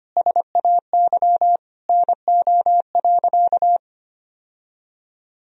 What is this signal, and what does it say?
Morse code, heard as a keyed tone, about 25 wpm, 700 Hz. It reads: SAY NO.
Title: Sentences from Top 1000 Words - Review Part 4 - 25wpm